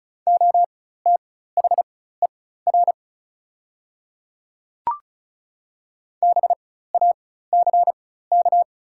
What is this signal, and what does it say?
Morse code, heard as a keyed tone, about 34 wpm, spaced as for 16 wpm, 700 Hz. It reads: OTHER BACK
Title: 4x Character Spacing - Top 200 Words - 35wpm